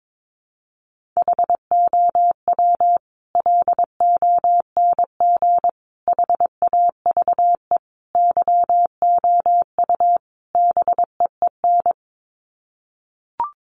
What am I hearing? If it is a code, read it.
Morse code, heard as a keyed tone, about 22 wpm, 700 Hz. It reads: HOW LONG HAVE YOU BEEN
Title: Sets of 5 Words Encore 22wpm